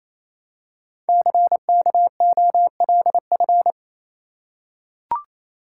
Morse code, heard as a keyed tone, about 28 wpm, 700 Hz. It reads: CKOLF
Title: Code Groups - 5 Letters - 28wpm